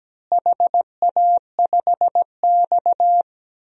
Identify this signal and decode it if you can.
Morse code, heard as a keyed tone, about 17 wpm, 700 Hz. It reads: HA5X
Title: DXCC-100 Call Signs - 4 Characters - Repeated 2x - 17wpm